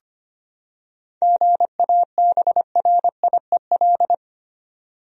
Morse code, heard as a keyed tone, about 25 wpm, 700 Hz. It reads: GABRIEL
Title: USA Common Names - 25wpm